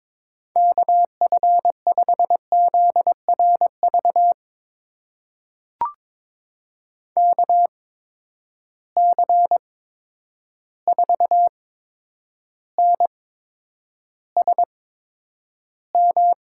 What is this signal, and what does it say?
Morse code, heard as a keyed tone, about 22 wpm, spaced as for 9 wpm, 700 Hz. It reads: KF5ZRV KC4NSM
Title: USA Call Signs - 6 Characters - 8x Character Spacing - 22wpm